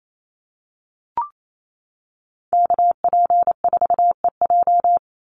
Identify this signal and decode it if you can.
Morse code, heard as a keyed tone, about 28 wpm, 700 Hz. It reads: KP4EJ
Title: Call Signs - 28wpm